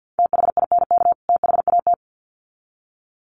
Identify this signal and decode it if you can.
Morse code, heard as a keyed tone, about 50 wpm, 700 Hz. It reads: THINK THAT
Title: Sets of 2 Words 50wpm